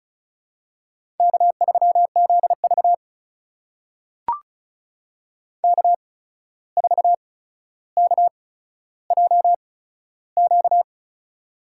Morse code, heard as a keyed tone, about 36 wpm, spaced as for 14 wpm, 700 Hz. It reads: K3ZV K4KJQ
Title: USA Call Signs - 8x Character Spacing - 35wpm